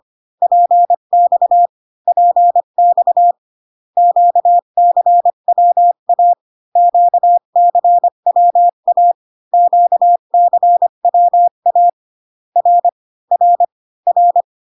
Morse code, heard as a keyed tone, about 25 wpm, 700 Hz. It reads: PX PX QCWA QCWA QCWA R R R